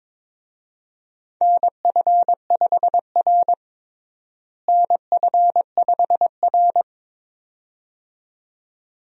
Morse code, heard as a keyed tone, about 21 wpm, 700 Hz. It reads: NF5R NF5R
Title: USA Call Signs - 4 Characters - Repeated 2x - 22wpm